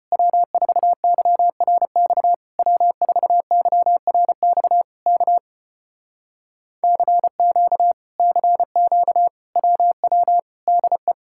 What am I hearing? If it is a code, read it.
Morse code, heard as a keyed tone, about 33 wpm, 700 Hz. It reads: W4YRX W4YRX K CQ CQ WW DE